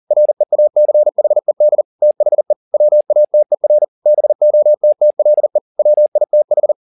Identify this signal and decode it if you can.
Morse code, heard as a keyed tone, about 40 wpm, 600 Hz. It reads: REACHED THE WATER BOTTLE WITH